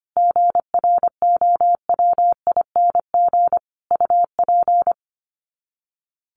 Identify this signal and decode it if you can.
Morse code, heard as a keyed tone, about 25 wpm, 700 Hz. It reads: GROWING UP
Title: Sets of 5 Words Encore 25wpm